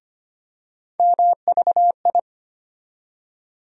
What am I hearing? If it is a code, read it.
Morse code, heard as a keyed tone, about 25 wpm, 700 Hz. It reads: MVI
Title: USA Call Sign Postfixes - 3 Letters 25wpm